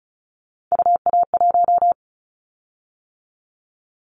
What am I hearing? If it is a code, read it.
Morse code, heard as a keyed tone, about 34 wpm, 700 Hz. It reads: UA1
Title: Callsign Prefixes for Top 100 DXCC - 35wpm